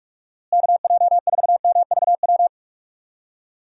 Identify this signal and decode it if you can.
Morse code, heard as a keyed tone, about 46 wpm, 700 Hz. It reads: KJ4MVW
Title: USA Call Signs - 6 Characters - Repeated 3x - 45wpm